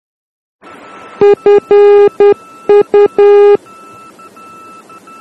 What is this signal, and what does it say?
Morse code, heard as a keyed tone, about 10 wpm, 410 Hz. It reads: FU
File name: germany / mx / FU-D